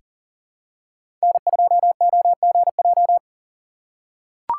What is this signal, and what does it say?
Morse code, heard as a keyed tone, about 40 wpm, 700 Hz. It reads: N2OGJ
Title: USA Call Signs - 5 Characters - 2x Character Spacing - 40wpm